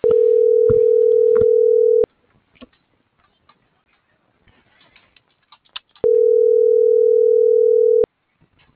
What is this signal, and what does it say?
Ringback tone (North America), 440 + 480 Hz.